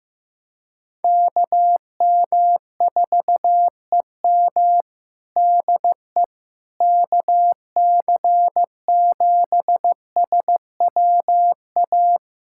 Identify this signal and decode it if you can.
Morse code, heard as a keyed tone, about 15 wpm, 700 Hz. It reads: KM4EM DE KC7SWA
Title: Practice QSO - Exchange 4 - Ending - 15wpm